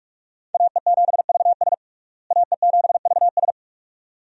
Morse code, heard as a keyed tone, about 42 wpm, 700 Hz. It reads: AE7VS AE7VS
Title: USA Call Signs - 5 Characters - Repeated 2x - 45wpm